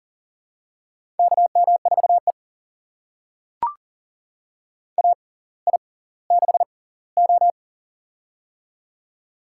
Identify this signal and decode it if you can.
Morse code, heard as a keyed tone, about 38 wpm, spaced as for 20 wpm, 690 Hz. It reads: KM4E AI6O
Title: USA Call Signs - 4 Characters - 6x Character Spacing - 40wpm